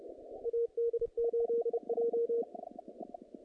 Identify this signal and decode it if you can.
Morse code, heard as a keyed tone, about 30 wpm, 470 Hz. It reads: AD8J
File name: contests / naqp-cw-k5tr-op-k5pi- -2300